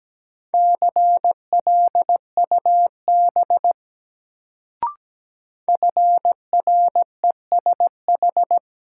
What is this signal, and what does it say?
Morse code, heard as a keyed tone, about 17 wpm, 700 Hz. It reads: CLUB FRESH
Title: Sentences from Top 1200 Words - New Word Drill Alt - 17wpm